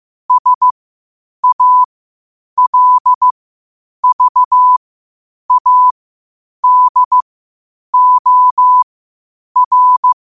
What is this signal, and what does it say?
Morse code, heard as a keyed tone, about 15 wpm, spaced as for 8 wpm, 1000 Hz. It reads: SALVADOR
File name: qotdfiles / 8wpm